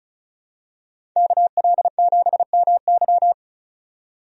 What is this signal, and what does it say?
Morse code, heard as a keyed tone, about 36 wpm, 700 Hz. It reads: KL7MY
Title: USA Call Signs - 5 Characters - 35wpm